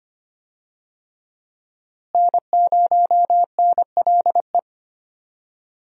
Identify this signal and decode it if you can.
Morse code, heard as a keyed tone, about 25 wpm, 700 Hz. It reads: N0NLE
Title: USA Call Signs - 5 Characters - Repeated 3x - 25wpm